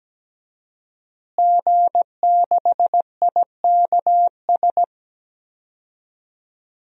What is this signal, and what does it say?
Morse code, heard as a keyed tone, about 17 wpm, 700 Hz. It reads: G6IKS
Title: DXCC-100 Call Signs - 5 Characters - Repeated 2x - 17wpm